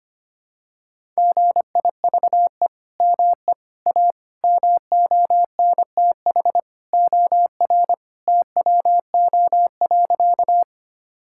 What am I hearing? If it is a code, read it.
Morse code, heard as a keyed tone, about 25 wpm, 700 Hz. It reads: GIVE ME A MONTH OR TWO.
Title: Sentences from Top 1000 Words - Review Part 5 - 25wpm